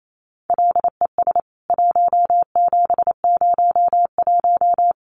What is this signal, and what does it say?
Morse code, heard as a keyed tone, about 28 wpm, 700 Hz. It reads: LES 1701
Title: CWT Exchange 28wpm